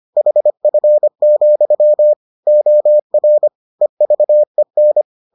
Morse code, heard as a keyed tone, about 25 wpm, 600 Hz. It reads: HF, OR EVEN